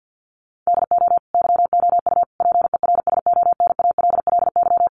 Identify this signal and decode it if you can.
Morse code, heard as a keyed tone, about 50 wpm, 700 Hz. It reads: DO YOU PERSONALLY
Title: Sentences from Top 1400 Words 50wpm